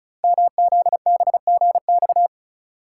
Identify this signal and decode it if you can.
Morse code, heard as a keyed tone, about 36 wpm, 690 Hz. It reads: MZBGX
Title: Code Groups - 5 Letters - 35wpm